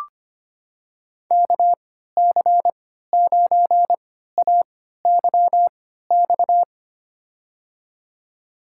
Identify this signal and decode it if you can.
Morse code, heard as a keyed tone, about 25 wpm, spaced as for 14 wpm, 700 Hz. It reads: KC9AYX
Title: USA Call Signs - 6 Characters - 3x Character Spacing - 25wpm